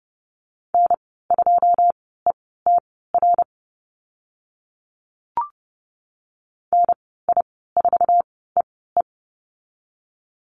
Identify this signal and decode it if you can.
Morse code, heard as a keyed tone, about 30 wpm, spaced as for 17 wpm, 700 Hz. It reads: N2ETR NI4EE
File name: Lesson 14g - Number 2 - Call Signs 30wpm